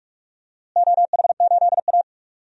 Morse code, heard as a keyed tone, about 46 wpm, 700 Hz. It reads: OH8A
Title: DXCC-100 Call Signs - 4 Characters - Repeated 2x - 45wpm